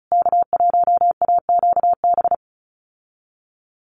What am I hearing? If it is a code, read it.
Morse code, heard as a keyed tone, about 34 wpm, 700 Hz. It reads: K1AQB
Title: USA Call Signs Repeated 3x 35wpm